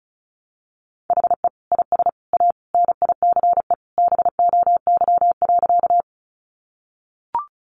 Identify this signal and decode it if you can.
Morse code, heard as a keyed tone, about 34 wpm, 700 Hz. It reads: HE IS A NICE BOY.
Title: Sentences from Top 1100 Words 35wpm